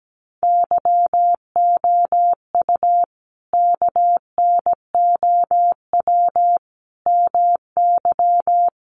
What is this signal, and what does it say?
Morse code, heard as a keyed tone, about 17 wpm, 700 Hz. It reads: YOU KNOW MY
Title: Sentences from Top 1200 Words 17wpm